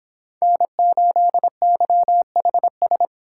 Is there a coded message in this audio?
Yes: Morse code from 0.4 to 3.1 s, heard as a keyed tone, about 31 wpm, 700 Hz. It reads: N8YHS